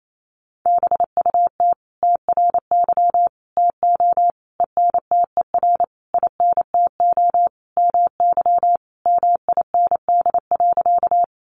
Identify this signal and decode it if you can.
Morse code, heard as a keyed tone, about 28 wpm, 700 Hz. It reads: BUT TRY TO ENTER INTO MY MIND.
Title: Sentences from Top 1300 Words 28wpm